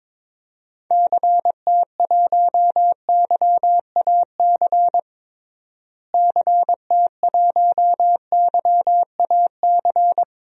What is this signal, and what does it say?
Morse code, heard as a keyed tone, about 21 wpm, 700 Hz. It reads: CT1YAC CT1YAC
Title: DXCC-100 Call Signs - 6 Characters - Repeated 2x - 22wpm